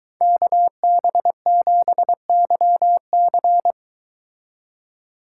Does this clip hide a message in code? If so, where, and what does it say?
Yes: Morse code from 0.2 to 3.7 s, heard as a keyed tone, about 22 wpm, 700 Hz. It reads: KB7YC